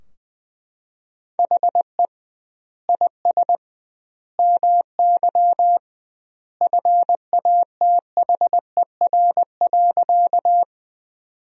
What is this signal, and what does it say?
Morse code, heard as a keyed tone, about 20 wpm, 700 Hz. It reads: HE IS MY FATHER.
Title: 2x Word-Spacing - Sentences from Top 300 Words - 20wpm